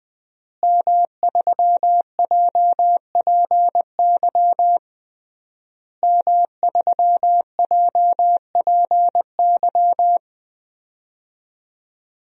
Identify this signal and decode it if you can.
Morse code, heard as a keyed tone, about 21 wpm, 700 Hz. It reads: M3JPY M3JPY
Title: DXCC-100 Call Signs - 5 Characters - Repeated 3x - 20wpm